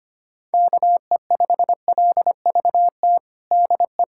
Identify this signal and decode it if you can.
Morse code, heard as a keyed tone, about 25 wpm, 700 Hz. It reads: KE5LVT DE
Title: USA Callsign DE Callsign - 6 Characters - 25wpm